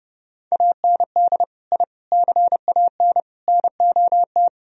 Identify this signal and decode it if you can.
Morse code, heard as a keyed tone, about 31 wpm, 700 Hz. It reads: AND I CAN NOT